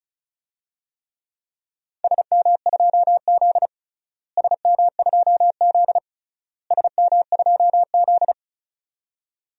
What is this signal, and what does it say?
Morse code, heard as a keyed tone, about 34 wpm, 700 Hz. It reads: SM2Z SM2Z SM2Z
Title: DXCC-100 Call Signs - 4 Characters - Repeated 3x - 35wpm